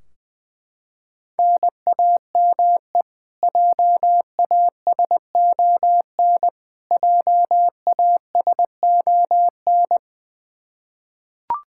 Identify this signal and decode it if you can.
Morse code, heard as a keyed tone, about 20 wpm, 700 Hz. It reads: NAME JASON JASON
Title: K1USN SST QSO Practice 20wpm